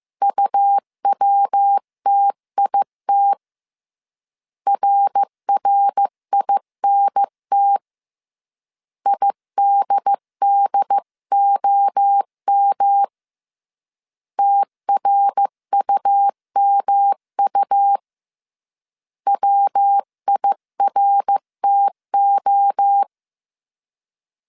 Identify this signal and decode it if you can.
Morse code, heard as a keyed tone, about 15 wpm, 800 Hz. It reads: UWTIT RRINT IDDOM TRUMU WIRTO
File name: CW / DET / 1 3 1500C S C 14wpm 2458